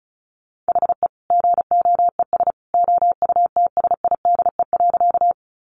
Morse code, heard as a keyed tone, about 34 wpm, 700 Hz. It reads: HE GOES OUTSIDE.
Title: Additional Easy Phrases 35wpm